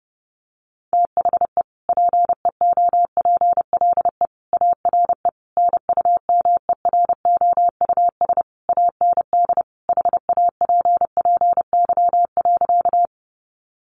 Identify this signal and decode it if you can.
Morse code, heard as a keyed tone, about 28 wpm, 700 Hz. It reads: THE PEOPLE ARE NUMEROUS AND HAPPY.
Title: Sentences from Top 1400 Words 30wpm